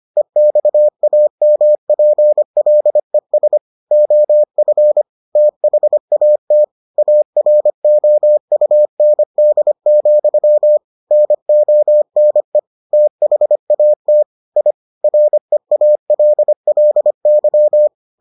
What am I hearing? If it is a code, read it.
Morse code, heard as a keyed tone, about 25 wpm, 600 Hz. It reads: EXAMPLES OF THAT AROUND, NONE THAT I REALLY